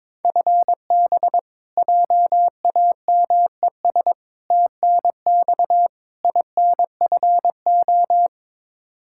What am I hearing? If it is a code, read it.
Morse code, heard as a keyed tone, about 21 wpm, 700 Hz. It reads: FB JAMES TNX INFO
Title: Practice QSO - Exchange 3 - More Info - 22wpm